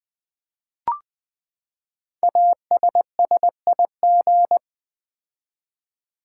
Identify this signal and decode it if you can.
Morse code, heard as a keyed tone, about 20 wpm, 700 Hz. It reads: ASSIG
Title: Build A Word - 5 Letters At A Time - 10-Letter Words - 20wpm